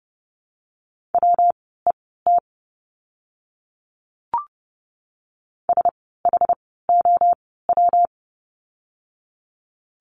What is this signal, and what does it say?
Morse code, heard as a keyed tone, about 30 wpm, spaced as for 17 wpm, 700 Hz. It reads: WET SHOW
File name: course / 30:10wpm / 115 Lesson 19f - Letter w - Words 30wpm